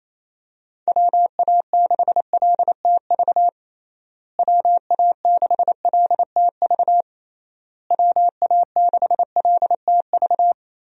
Morse code, heard as a keyed tone, about 28 wpm, 700 Hz. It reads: WA6LTV WA6LTV WA6LTV
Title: USA Call Signs - 6 Characters - Repeated 3x - 28wpm